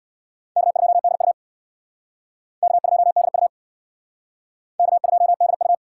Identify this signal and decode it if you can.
Morse code, heard as a keyed tone, about 63 wpm, 700 Hz. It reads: K2DU K2DU K2DU